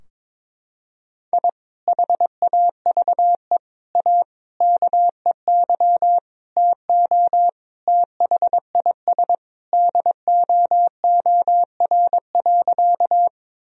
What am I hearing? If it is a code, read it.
Morse code, heard as a keyed tone, about 22 wpm, 700 Hz. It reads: I HAVE A KEY TO THIS DOOR.
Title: Sentences from Top 1000 Words - Review Part 4 - 22wpm